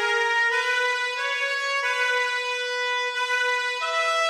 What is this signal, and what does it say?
Emergency Alert System (SAME) tones, heard as harsh buzzing bursts. It is heard from 0.6 to 3.8 s.